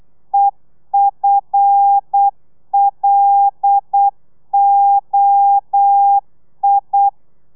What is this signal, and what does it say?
Morse code, heard as a keyed tone, about 8 wpm, 800 Hz. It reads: EFLOI
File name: telegrafia / i2tfj / Lez5 35